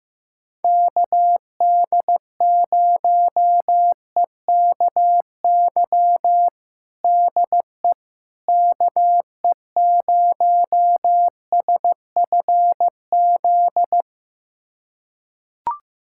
Morse code, heard as a keyed tone, about 15 wpm, 700 Hz. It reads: KD0EKY DE KE0SFZ
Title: USA Callsign DE Callsign - 6 Characters - 15wpm